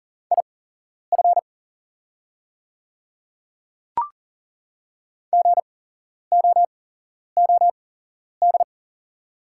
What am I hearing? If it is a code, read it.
Morse code, heard as a keyed tone, about 40 wpm, spaced as for 11 wpm, 690 Hz. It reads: IF GOOD
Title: 8x Character Spacing - Top 200 Words - 40wpm